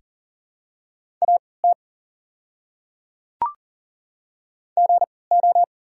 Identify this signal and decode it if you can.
Morse code, heard as a keyed tone, about 40 wpm, spaced as for 23 wpm, 700 Hz. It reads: AT GO
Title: 3x Character Spacing - Top 100 Words - 40wpm